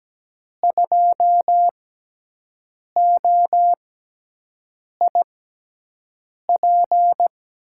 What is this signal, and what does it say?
Morse code, heard as a keyed tone, about 17 wpm, spaced as for 6 wpm, 700 Hz. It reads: 2OIP
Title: USA Call Signs - 5 Characters - 6x Character Spacing - 17wpm